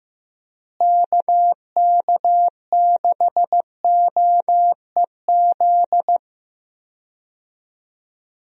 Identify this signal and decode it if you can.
Morse code, heard as a keyed tone, about 15 wpm, 700 Hz. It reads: KK6OEZ
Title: USA Call Signs - 6 Characters - Repeated 3x - 15wpm